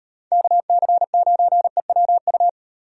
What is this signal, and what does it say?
Morse code, heard as a keyed tone, about 36 wpm, 700 Hz. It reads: KC9EWU